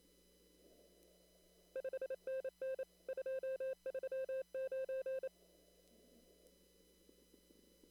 Morse code, heard as a keyed tone, about 27 wpm, 530 Hz. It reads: 5NN 239
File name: CW / kt5j wpx cw-op-k5pi